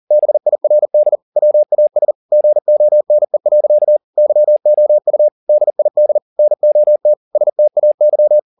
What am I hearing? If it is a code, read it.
Morse code, heard as a keyed tone, about 40 wpm, 600 Hz. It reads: BIRD WAS GONE. YOU DID NOT STAY